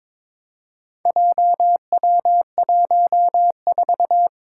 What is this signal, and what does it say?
Morse code, heard as a keyed tone, about 22 wpm, 700 Hz. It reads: JW14
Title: Four Challenging Letter-Number Combinations 22wpm